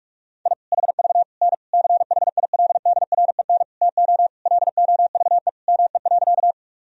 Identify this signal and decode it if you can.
Morse code, heard as a keyed tone, about 46 wpm, 700 Hz. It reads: I HV N CHILDREN TO LOVE ME.
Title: Sentences from Top 400 Words and Abbreviations 45wpm